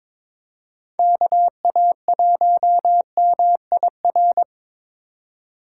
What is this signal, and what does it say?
Morse code, heard as a keyed tone, about 26 wpm, 700 Hz. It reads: KA1MIR
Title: Speed Racing - USA 2x3 Call Signs 22wpm